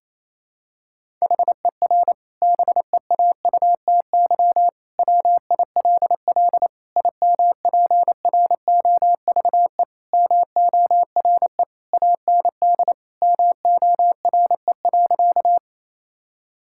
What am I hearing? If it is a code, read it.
Morse code, heard as a keyed tone, about 28 wpm, 700 Hz. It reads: HER BEAUTY WILL IMPROVE MORE AND MORE.
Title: Sentences from Top 1000 Words - Review Part 5 - 28wpm